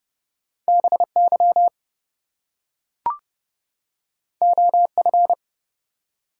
Morse code, heard as a keyed tone, about 29 wpm, 700 Hz. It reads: BY OF
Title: Two Letter Words - 30wpm